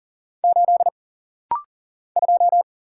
Morse code, heard as a keyed tone, about 40 wpm, 700 Hz. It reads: 8 2
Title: Single Letter-Number Alt - Rapid-Fire 40wpm